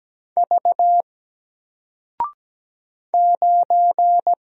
Morse code, heard as a keyed tone, about 17 wpm, 700 Hz. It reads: V 9
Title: Single Letter-Number Alt 17wpm